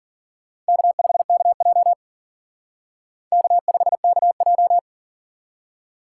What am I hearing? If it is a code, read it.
Morse code, heard as a keyed tone, about 49 wpm, 700 Hz. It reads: K5KJ K5KJ
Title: Speed Racing - USA 1x2 Call Signs 40wpm